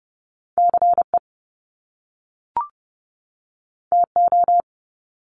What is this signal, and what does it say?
Morse code, heard as a keyed tone, about 30 wpm, 700 Hz. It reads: CE TO
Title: Top Two Letter Combinations in Words 30wpm